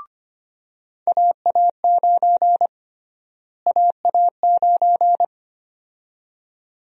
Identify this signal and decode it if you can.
Morse code, heard as a keyed tone, about 25 wpm, 700 Hz. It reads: AA9 AA9
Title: USA Call Sign Prefixes - Repeated 2x - 25wpm